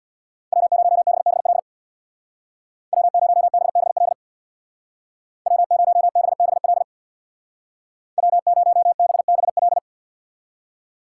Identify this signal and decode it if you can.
Morse code, heard as a keyed tone, about 60 wpm, 700 Hz. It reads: W0BBL W0BBL W0BBL W0BBL
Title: Speed Racing - USA 1x3 Call Signs 50wpm